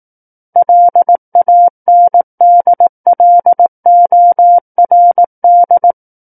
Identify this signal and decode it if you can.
Morse code, heard as a keyed tone, about 18 wpm, 700 Hz. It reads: LANDLORD
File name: 18 WPM